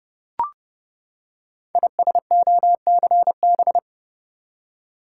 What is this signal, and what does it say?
Morse code, heard as a keyed tone, about 30 wpm, 700 Hz. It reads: ISOCB
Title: Code Groups - 5 Letters - 30wpm